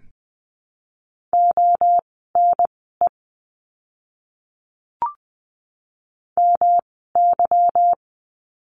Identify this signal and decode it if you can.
Morse code, heard as a keyed tone, about 20 wpm, spaced as for 15 wpm, 700 Hz. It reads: ONE MY
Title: 237 Lesson 38f - Letter q - Words 20wpm